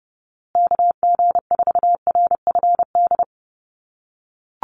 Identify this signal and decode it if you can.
Morse code, heard as a keyed tone, about 30 wpm, 700 Hz. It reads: KG4RFD
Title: USA Call Signs Repeated 3x 30wpm